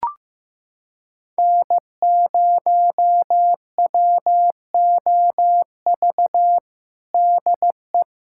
Morse code, heard as a keyed tone, about 16 wpm, 700 Hz. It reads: N0WOV DE
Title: USA Callsign DE Callsign - 5 Characters - 15wpm